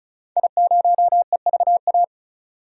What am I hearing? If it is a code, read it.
Morse code, heard as a keyed tone, about 35 wpm, 700 Hz. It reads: I0EVA
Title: DXCC-100 Call Signs - 5 Characters - 35wpm